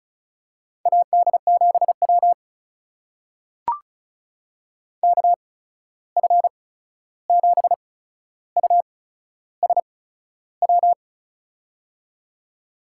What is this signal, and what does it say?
Morse code, heard as a keyed tone, about 36 wpm, spaced as for 13 wpm, 700 Hz. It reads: AD7W KF7USW
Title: USA Call Signs - 8x Character Spacing - 35wpm